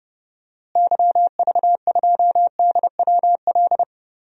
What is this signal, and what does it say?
Morse code, heard as a keyed tone, about 32 wpm, 700 Hz. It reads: YV2DWL
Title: DXCC-100 Call Signs - 6 Characters - Repeated 3x - 30wpm